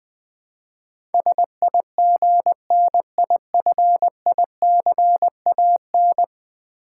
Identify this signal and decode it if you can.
Morse code, heard as a keyed tone, about 21 wpm, 700 Hz. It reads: SIGNIFICAN
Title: Build A Word - 5 Letters At A Time - 12-Letter Words - 20wpm